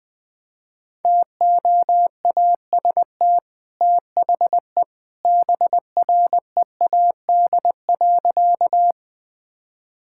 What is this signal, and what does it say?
Morse code, heard as a keyed tone, about 20 wpm, 700 Hz. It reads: TOAST THE BREAD.
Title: Easy Sentences with Hints - 20wpm